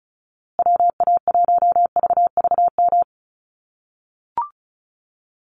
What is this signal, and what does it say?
Morse code, heard as a keyed tone, about 34 wpm, 700 Hz. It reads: WA1VVM
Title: USA Call Signs Repeated 2x 35wpm